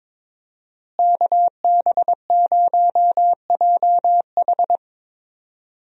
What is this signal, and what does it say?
Morse code, heard as a keyed tone, about 22 wpm, 700 Hz. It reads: KB0JH
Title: USA Call Signs - 5 Characters - Repeated 3x - 22wpm